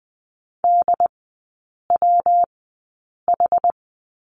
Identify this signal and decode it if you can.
Morse code, heard as a keyed tone, about 20 wpm, spaced as for 8 wpm, 700 Hz. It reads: DWH
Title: 127 Lesson 21e - Number 6 - Three Characters 20wpm